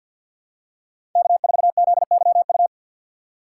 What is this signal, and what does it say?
Morse code, heard as a keyed tone, about 53 wpm, 700 Hz. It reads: K4ZYU
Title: USA Call Signs - 5 Characters - 6x Character Spacing - 50wpm